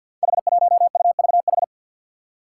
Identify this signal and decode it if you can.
Morse code, heard as a keyed tone, about 46 wpm, 700 Hz. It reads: H1UVH
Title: Five Challenging Letter-Number Combinations 50wpm